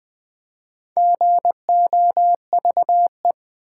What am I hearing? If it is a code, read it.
Morse code, heard as a keyed tone, about 20 wpm, 700 Hz. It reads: GOVE
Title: Build A Word - 4 Letters At A Time - 10-Letter Words - 20wpm